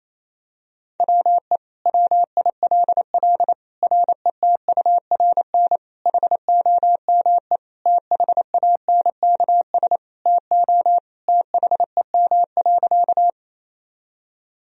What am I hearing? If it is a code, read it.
Morse code, heard as a keyed tone, about 28 wpm, 700 Hz. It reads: WE WILL RETURN HOME THANKS TO THEM.
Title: Sentences from Top 1000 Words - Review Part 3 - 28wpm